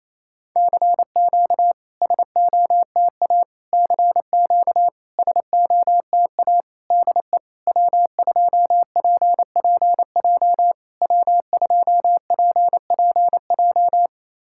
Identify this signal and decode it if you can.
Morse code, heard as a keyed tone, about 28 wpm, 700 Hz. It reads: CQ SOTA CQ SOTA DE W2PPJ W2PPJ
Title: SOTA QSO Practice 28wpm